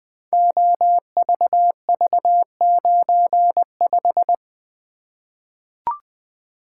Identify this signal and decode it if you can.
Morse code, heard as a keyed tone, about 21 wpm, 700 Hz. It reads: OVV95
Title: Code Groups - 5 Letter:Numbers - 20wpm